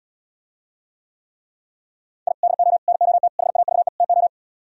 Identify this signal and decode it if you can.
Morse code, heard as a keyed tone, about 77 wpm, 700 Hz. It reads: I DO NOT HAVE TO